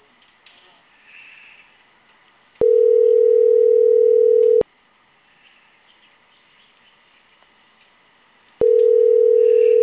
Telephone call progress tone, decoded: ringback tone

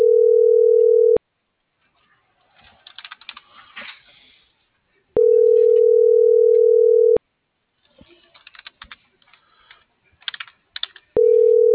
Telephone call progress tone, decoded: ringback tone

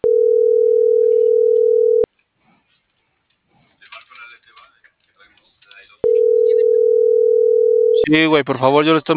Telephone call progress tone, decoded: ringback tone

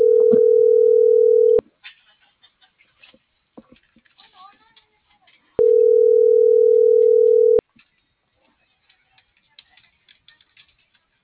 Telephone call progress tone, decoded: ringback tone

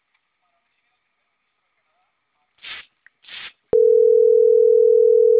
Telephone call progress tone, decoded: ringback tone

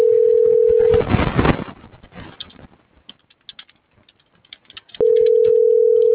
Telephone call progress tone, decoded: ringback tone

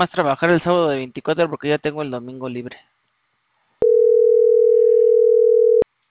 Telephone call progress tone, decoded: ringback tone

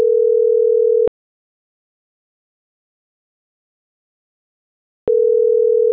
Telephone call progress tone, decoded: ringback tone

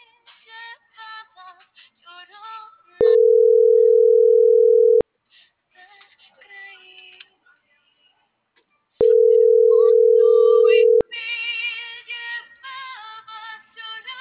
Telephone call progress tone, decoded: ringback tone

